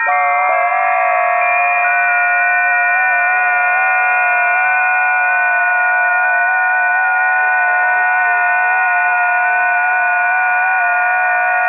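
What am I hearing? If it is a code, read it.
FT8.
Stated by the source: FT8 test recording of a busy 20 m band, from the ft8_lib test set by Karlis Goba.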